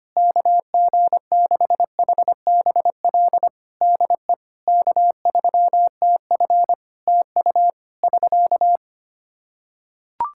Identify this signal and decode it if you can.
Morse code, heard as a keyed tone, about 25 wpm, 700 Hz. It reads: KG6HBL DE K3TF TU <SK>
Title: Practice QSO - Exchange 4 - Ending - 25wpm